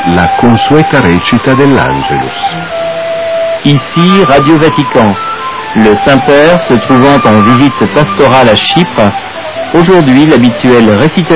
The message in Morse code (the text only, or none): none